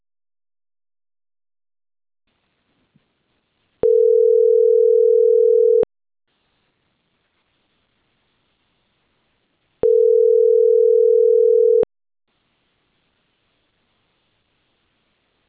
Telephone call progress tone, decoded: ringback tone